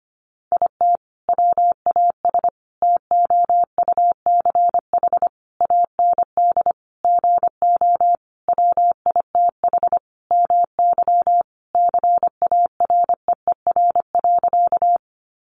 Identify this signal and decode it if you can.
Morse code, heard as a keyed tone, about 25 wpm, 700 Hz. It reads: IT WAS TOUCH AND GO WITH MY CAREER.
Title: Sentences from Top 1200 Words 25wpm